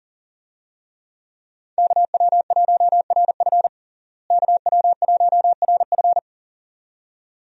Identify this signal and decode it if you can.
Morse code, heard as a keyed tone, about 37 wpm, 700 Hz. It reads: KW1RF KW1RF
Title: USA Call Signs - 5 Characters - Repeated 2x - 40wpm